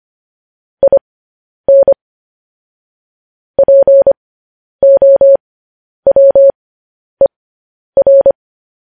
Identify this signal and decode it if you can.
Morse code, heard as a keyed tone, about 25 wpm, spaced as for 10 wpm, 550 Hz. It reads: IN POWER
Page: News Headlines in Morse Code at 10 WPM